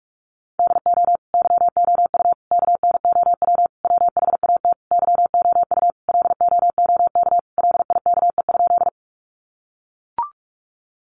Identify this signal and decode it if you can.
Morse code, heard as a keyed tone, about 44 wpm, 700 Hz. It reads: DO YOU KNOW WHAT YOU LOOK LIKE?
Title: Sentences from Top 100 Words 45wpm